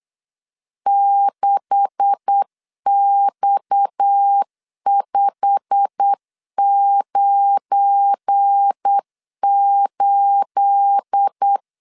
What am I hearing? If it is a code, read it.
Morse code, heard as a keyed tone, about 8 wpm, 800 Hz. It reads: 6X598
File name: LEZIONI / 9 286 8wpm 3425